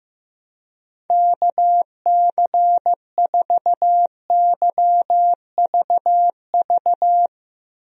Morse code, heard as a keyed tone, about 15 wpm, 700 Hz. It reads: KC4YVV
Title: USA Call Signs - 6 Characters - Repeated 3x - 15wpm